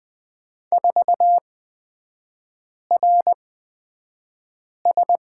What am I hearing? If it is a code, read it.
Morse code, heard as a keyed tone, about 20 wpm, spaced as for 5 wpm, 700 Hz. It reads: 4RS
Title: 229 Lesson 37e - Letter x - Three Characters 20wpm